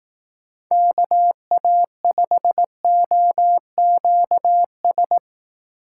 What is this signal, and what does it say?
Morse code, heard as a keyed tone, about 18 wpm, 700 Hz. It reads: KA5OQS
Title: Speed Racing - USA 2x3 Call Signs 15wpm